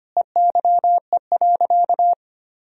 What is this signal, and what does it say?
Morse code, heard as a keyed tone, about 25 wpm, 700 Hz. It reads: EYE.
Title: Sentences from Top 1000 Words - Review Part 1 - 25wpm